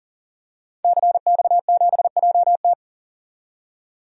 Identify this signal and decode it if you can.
Morse code, heard as a keyed tone, about 40 wpm, 700 Hz. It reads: CX7JT
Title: DXCC-100 Call Signs - 5 Characters - 40wpm